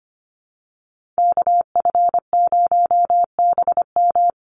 Morse code, heard as a keyed tone, about 25 wpm, 700 Hz. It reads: KF0BM
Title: USA Call Signs Repeated 2x 25wpm